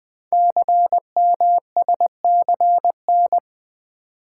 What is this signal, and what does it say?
Morse code, heard as a keyed tone, about 20 wpm, 700 Hz. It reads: CMSCN
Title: Code Groups - 5 Letters - 20wpm